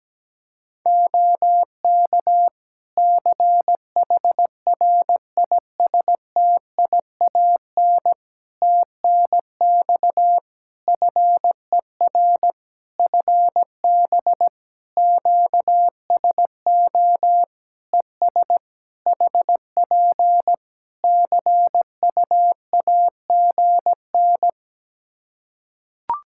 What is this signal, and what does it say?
Morse code, heard as a keyed tone, about 17 wpm, 700 Hz. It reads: OK CHRISTIAN TNX FER FB QSO ES HP CUAGN
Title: Practice QSO - Exchange 4 - Ending - 17wpm